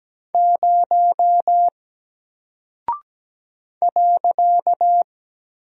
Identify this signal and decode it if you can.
Morse code, heard as a keyed tone, about 17 wpm, 700 Hz. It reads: 0 .
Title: Single Character (Letter:Number:Punctuation:Prosign) Alt 17wpm